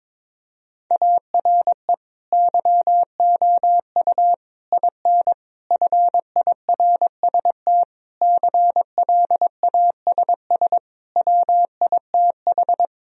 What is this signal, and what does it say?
Morse code, heard as a keyed tone, about 22 wpm, 700 Hz. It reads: ARE YOU IN FIRST CLASS WITH
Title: Sentences from Top 1000 Words - Review Part 2 - 22wpm